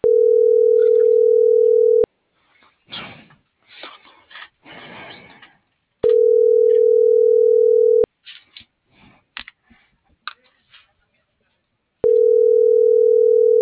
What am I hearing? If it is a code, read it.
Ringback tone (North America), 440 + 480 Hz.